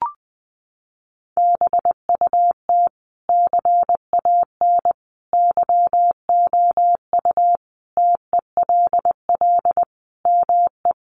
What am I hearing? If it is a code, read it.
Morse code, heard as a keyed tone, about 20 wpm, 700 Hz. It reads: BUT CAN YOU TELL ME
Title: Sentences from Top 1400 Words 20wpm